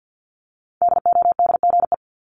Morse code, heard as a keyed tone, about 50 wpm, 700 Hz. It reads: DODGE
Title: Multi-letter Phonemes 50wpm